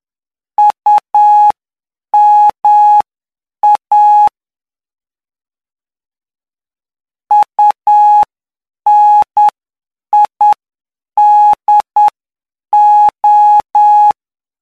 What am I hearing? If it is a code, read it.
Morse code, heard as a keyed tone, about 9 wpm, spaced as for 7 wpm, 840 Hz. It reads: UMA UNIDO
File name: cw / py2lcd / 04-PY2LCD-Li��o 13